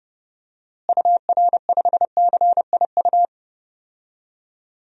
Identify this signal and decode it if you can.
Morse code, heard as a keyed tone, about 32 wpm, 700 Hz. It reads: UR5CIU
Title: DXCC-100 Call Signs - 6 Characters - Repeated 3x - 30wpm